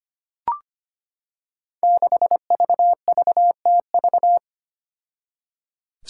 Morse code, heard as a keyed tone, about 25 wpm, 700 Hz. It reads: 6VVTV
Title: Five Challenging Letter-Number Combinations 25wpm